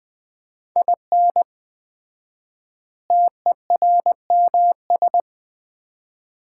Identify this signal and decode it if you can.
Morse code, heard as a keyed tone, about 20 wpm, 700 Hz. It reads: IN TERMS
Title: 4x Word-Spacing - Sets of 4 Words - 20wpm